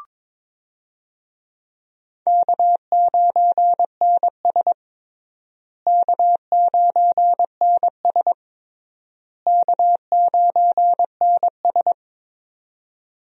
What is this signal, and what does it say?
Morse code, heard as a keyed tone, about 22 wpm, 700 Hz. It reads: K9NS K9NS K9NS
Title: USA Call Signs - 4 Characters - Repeated 3x - 22wpm